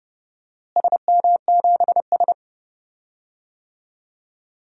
Morse code, heard as a keyed tone, about 30 wpm, 700 Hz. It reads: SM7S
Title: DXCC-100 Call Signs - 4 Characters - Repeated 2x - 30wpm